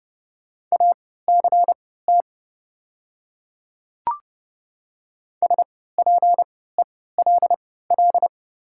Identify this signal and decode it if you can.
Morse code, heard as a keyed tone, about 31 wpm, spaced as for 17 wpm, 700 Hz. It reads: ACT SPELL
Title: 3x Character Spacing - Top 500 Words - 30wpm